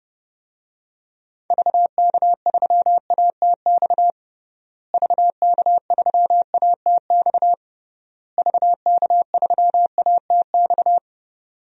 Morse code, heard as a keyed tone, about 30 wpm, 700 Hz. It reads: VK3ATX VK3ATX VK3ATX
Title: DXCC-100 Call Signs - 6 Characters - Repeated 3x - 30wpm